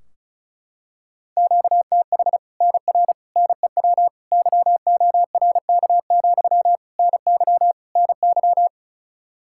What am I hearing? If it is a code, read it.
Morse code, heard as a keyed tone, about 34 wpm, 700 Hz. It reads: QTH NR NEW YORK, NY NY
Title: Practice QSO - Exchange 1 - Beginning - 35wpm